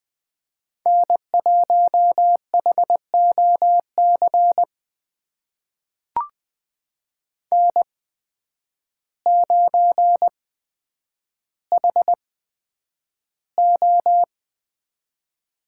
Morse code, heard as a keyed tone, about 20 wpm, spaced as for 9 wpm, 700 Hz. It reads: N1HOC N9HO
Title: USA Call Signs - 5 Characters - 8x Character Spacing - 20wpm